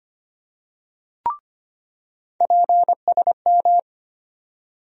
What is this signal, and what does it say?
Morse code, heard as a keyed tone, about 25 wpm, 700 Hz. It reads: PSM
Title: USA Call Sign Postfixes - 3 Letters 25wpm